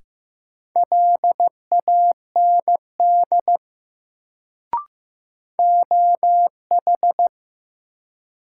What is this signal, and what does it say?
Morse code, heard as a keyed tone, about 15 wpm, 700 Hz. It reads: LAND OH
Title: Sentences from Top 500 Words - New Word Drill - 15wpm